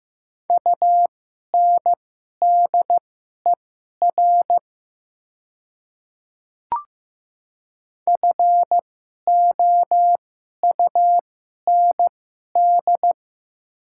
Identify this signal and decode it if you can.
Morse code, heard as a keyed tone, about 15 wpm, spaced as for 11 wpm, 700 Hz. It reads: UNDER FOUND
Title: 2x Character Spacing - Top 300 Words - 15wpm